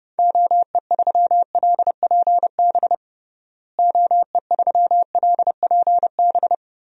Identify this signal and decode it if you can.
Morse code, heard as a keyed tone, about 32 wpm, 700 Hz. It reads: OE3LPB OE3LPB
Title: DXCC-100 Call Signs - 6 Characters - Repeated 2x - 30wpm